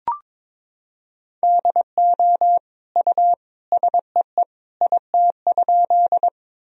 Morse code, heard as a keyed tone, about 22 wpm, 700 Hz. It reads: DO U SEE IT?
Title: Sentences from Top 100 Words and Abbreviations 22wpm